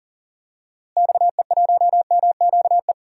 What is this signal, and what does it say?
Morse code, heard as a keyed tone, about 40 wpm, 700 Hz. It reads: XE1MQE
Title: DXCC-100 Call Signs - 6 Characters - Repeated 3x - 40wpm